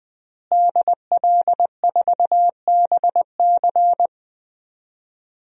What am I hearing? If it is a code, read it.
Morse code, heard as a keyed tone, about 20 wpm, 700 Hz. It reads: DL4BC
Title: DXCC-100 Call Signs - 5 Characters - Repeated 2x - 20wpm